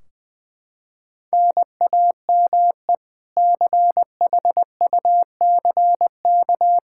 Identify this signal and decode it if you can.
Morse code, heard as a keyed tone, about 20 wpm, 700 Hz. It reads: NAME CHUCK
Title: K1USN SST QSO Practice 20wpm